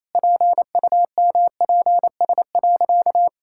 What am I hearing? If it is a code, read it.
Morse code, heard as a keyed tone, about 28 wpm, 700 Hz. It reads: PUMPS.
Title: Pangram Challenge v2 - 28wpm